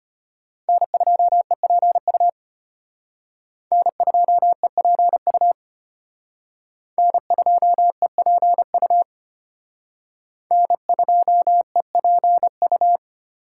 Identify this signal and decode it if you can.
Morse code, heard as a keyed tone, about 31 wpm, 700 Hz. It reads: N2EPU N2EPU N2EPU N2EPU